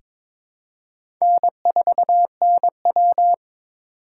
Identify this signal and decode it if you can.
Morse code, heard as a keyed tone, about 22 wpm, 700 Hz. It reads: N4NW
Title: K1USN SST QSO Practice - 22wpm